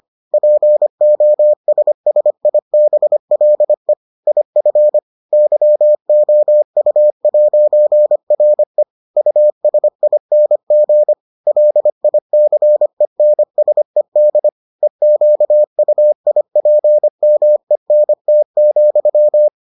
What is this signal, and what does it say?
Morse code, heard as a keyed tone, about 25 wpm, 600 Hz. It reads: POSSIBLE IF YOU'RE USING LICENSED EQUIPMENT,